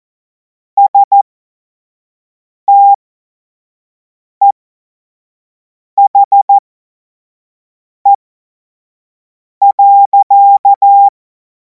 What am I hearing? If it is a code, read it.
Morse code, heard as a keyed tone, about 14 wpm, spaced as for 5 wpm, 800 Hz. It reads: STEHE.
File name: Morsekurs / Witze / Witz 02. Juni 2026 25 F70